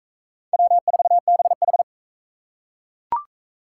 Morse code, heard as a keyed tone, about 39 wpm, 700 Hz. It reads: W4BH